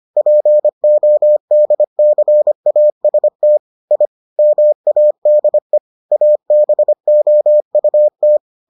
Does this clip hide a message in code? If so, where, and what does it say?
Yes: Morse code from 0.2 to 8.4 s, heard as a keyed tone, about 25 wpm, 600 Hz. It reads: PODCAST I MADE ABOUT